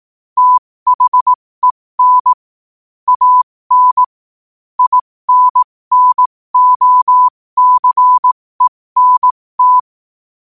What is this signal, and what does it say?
Morse code, heard as a keyed tone, about 18 wpm, spaced as for 15 wpm, 1000 Hz. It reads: THEN AN INNOCENT
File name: qotdfiles / 15wpm